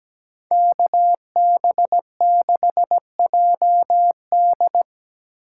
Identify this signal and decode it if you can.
Morse code, heard as a keyed tone, about 20 wpm, 700 Hz. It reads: KB6JD
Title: Speed Racing - USA 2x2 Call Signs 17wpm